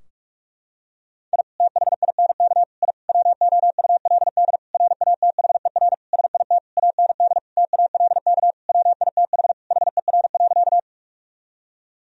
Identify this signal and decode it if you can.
Morse code, heard as a keyed tone, about 46 wpm, 700 Hz. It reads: I THINK I WOULD RATHER SIT AND TALK WITH HER.
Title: Sentences from Top 1000 Words - Review Part 4 - 45wpm